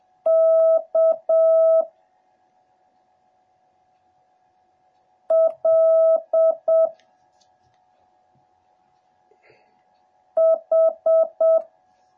Morse code, heard as a keyed tone, about 7 wpm, 650 Hz. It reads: K L H